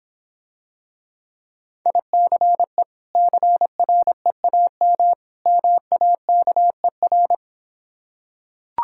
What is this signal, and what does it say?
Morse code, heard as a keyed tone, about 26 wpm, 700 Hz. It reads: ICE CREAM MAKER